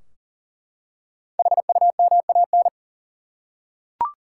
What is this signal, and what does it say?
Morse code, heard as a keyed tone, about 40 wpm, 700 Hz. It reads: HUMAN
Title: Five Letter Words - 40wpm